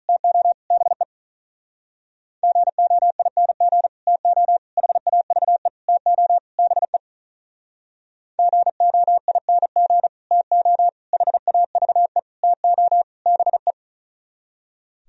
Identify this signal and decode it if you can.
Morse code, heard as a keyed tone, about 42 wpm, 700 Hz. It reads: TO BE GOING TO HAVE TO BE GOING TO HAVE TO BE
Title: Speed-Racing - Sets of 5 Words 35wpm